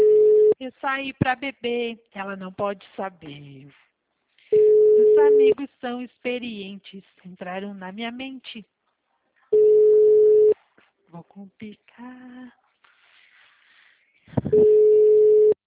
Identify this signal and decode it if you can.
Ringback tone (Europe, 425 Hz), with speech.